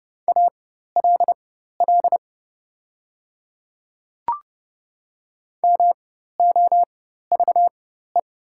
Morse code, heard as a keyed tone, about 30 wpm, spaced as for 14 wpm, 690 Hz. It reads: ALL MOVE
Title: 4x Character Spacing - Top 200 Words - 30wpm